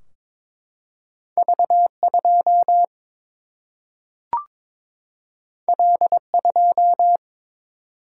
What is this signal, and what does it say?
Morse code, heard as a keyed tone, about 22 wpm, 700 Hz. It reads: V 2 L 2
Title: Single-Letter Single-Number 22wpm